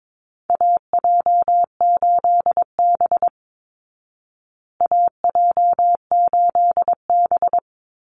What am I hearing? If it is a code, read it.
Morse code, heard as a keyed tone, about 22 wpm, 700 Hz. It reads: AJ8B AJ8B
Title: Call Signs Repeated 3x - 22wpm